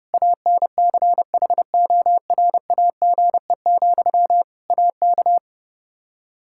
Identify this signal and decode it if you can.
Morse code, heard as a keyed tone, about 30 wpm, 700 Hz. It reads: ANCHORAGE, AK
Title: K1USN SST QSO Practice 30wpm